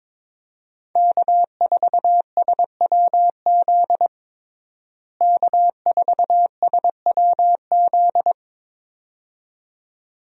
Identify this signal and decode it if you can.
Morse code, heard as a keyed tone, about 22 wpm, 700 Hz. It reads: K4SWZ K4SWZ
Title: USA Call Signs - 5 Characters - Repeated 2x - 22wpm